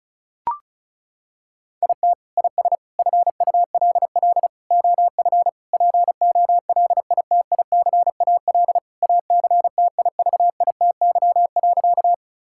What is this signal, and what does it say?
Morse code, heard as a keyed tone, about 36 wpm, 700 Hz. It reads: IT IS FULL OF POLITICAL ACTIVITY.
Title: Sentences from Top 1000 Words - Review Part 5 - 35wpm